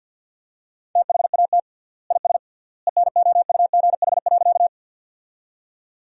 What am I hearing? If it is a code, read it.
Morse code, heard as a keyed tone, about 53 wpm, 700 Hz. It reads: THAT IS ENOUGH.
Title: 3x Word-Spacing - Sentences from Top 400 Words - 50wpm